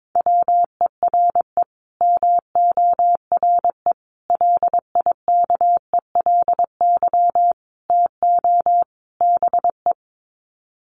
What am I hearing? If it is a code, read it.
Morse code, heard as a keyed tone, about 22 wpm, 700 Hz. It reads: WERE MORE LIKELY TO BE
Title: Sets of 5 Words Encore2 22wpm